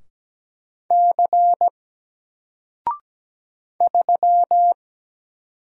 Morse code, heard as a keyed tone, about 17 wpm, 700 Hz. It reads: C 3
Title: Single Letter-Number 17wpm